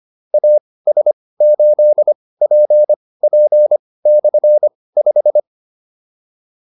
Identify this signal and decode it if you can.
Morse code, heard as a keyed tone, about 25 wpm, spaced as for 18 wpm, 600 Hz. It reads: AS8PP/5
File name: QSO302 18